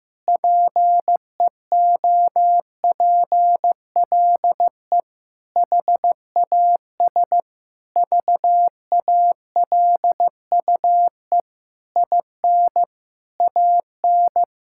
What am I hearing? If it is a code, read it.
Morse code, heard as a keyed tone, about 15 wpm, 700 Hz. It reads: PEOPLE HAS VALUE IN AN
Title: Sentences from Top 1000 Words - Review Part 6 - 15wpm